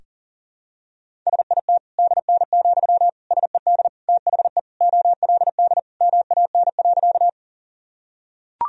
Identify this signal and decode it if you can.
Morse code, heard as a keyed tone, about 37 wpm, 700 Hz. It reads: SIT DN, SED THE OLD MAN.
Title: Sentences from Top 500 Words and Abbreviations 40wpm